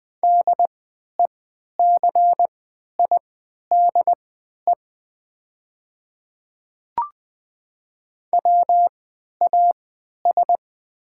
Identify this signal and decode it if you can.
Morse code, heard as a keyed tone, about 20 wpm, spaced as for 11 wpm, 700 Hz. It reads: DECIDE WAS
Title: 3x Character Spacing - Top 500 Words - 20wpm